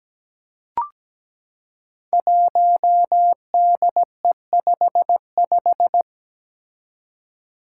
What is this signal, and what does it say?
Morse code, heard as a keyed tone, about 17 wpm, 700 Hz. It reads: 1DE55
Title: Five Challenging Letter-Number Combinations 17wpm